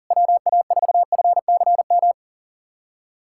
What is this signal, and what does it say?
Morse code, heard as a keyed tone, about 44 wpm, 700 Hz. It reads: WA4FCM
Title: SOTA QSO Practice 40wpm